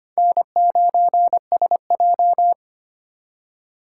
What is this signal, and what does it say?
Morse code, heard as a keyed tone, about 25 wpm, 700 Hz. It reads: N9SJ
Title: USA Call Signs - 4 Characters - 2x Character Spacing - 25wpm